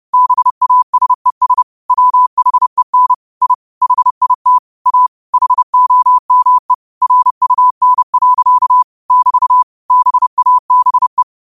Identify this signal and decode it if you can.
Morse code, heard as a keyed tone, about 30 wpm, 1000 Hz. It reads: BASES WHEN I HIT A HOME RUN. <BT> BABE
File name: qotdfiles / 30wpm